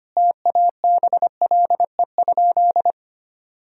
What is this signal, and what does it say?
Morse code, heard as a keyed tone, about 25 wpm, 700 Hz. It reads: TABLE?
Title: Sentences from Top 1000 Words - Review Part 3 - 25wpm